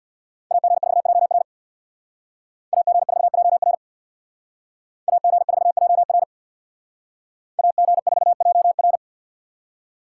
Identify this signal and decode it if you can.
Morse code, heard as a keyed tone, about 59 wpm, 700 Hz. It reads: AG4JR AG4JR AG4JR AG4JR
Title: Speed Racing - USA 2x2 Call Signs 50wpm